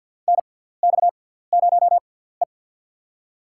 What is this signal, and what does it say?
Morse code, heard as a keyed tone, about 52 wpm, spaced as for 17 wpm, 700 Hz. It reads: NX0E
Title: USA Call Signs - 6x Character Spacing - 50wpm